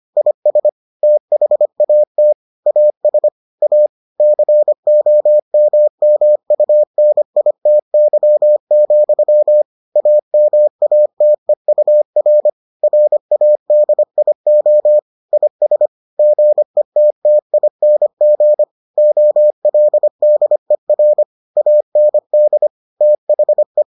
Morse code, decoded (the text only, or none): IS THAT AS A COMMUNITY, AMATEUR RADIO IS GETTING OLDER AND THE